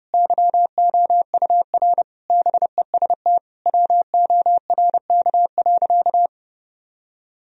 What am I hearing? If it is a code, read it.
Morse code, heard as a keyed tone, about 30 wpm, 700 Hz. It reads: YOUR BEST WORK.
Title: Sentences from Top 1000 Words - Review Part 3 - 30wpm